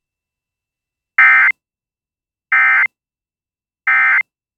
Emergency Alert System (SAME) tones, heard as harsh buzzing bursts. It is heard from 1.2 to 4.2 s.